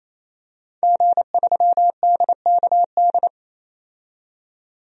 Morse code, heard as a keyed tone, about 28 wpm, 700 Hz. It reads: G3DKD
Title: DXCC-100 Call Signs - 5 Characters - 28wpm